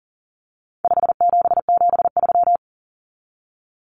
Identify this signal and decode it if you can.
Morse code, heard as a keyed tone, about 40 wpm, 700 Hz. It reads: 5773
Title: Common Cut Numbers - 4-Digit 40wpm